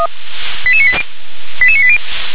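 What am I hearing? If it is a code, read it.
Five-tone selective call (ZVEI), heard as quick runs of five tones, tones of about 70 ms. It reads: 88980 88980
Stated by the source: Five-tone ZVEI selective call received on SDR.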